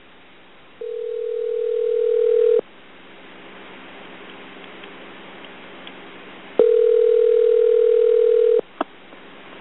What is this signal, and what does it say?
Ringback tone (North America), 440 + 480 Hz.